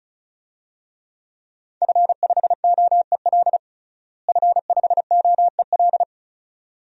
Morse code, heard as a keyed tone, about 36 wpm, 700 Hz. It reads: F5OEL F5OEL
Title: DXCC-100 Call Signs - 5 Characters - Repeated 2x - 35wpm